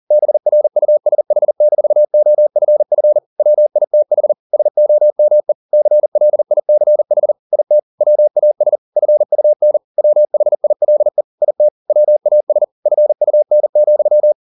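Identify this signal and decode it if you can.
Morse code, heard as a keyed tone, about 40 wpm, 600 Hz. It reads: BRUSH-OFF WITH SOME CLICH IT WAS FUN WHILE IT WAS FUN,
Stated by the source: Morse code at 40 wpm, practice text, CW med Gnister.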